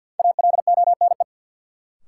Morse code, heard as a keyed tone, about 46 wpm, 700 Hz. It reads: ALONE
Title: Sentences from Top 1000 Words - New Word Drill - 50wpm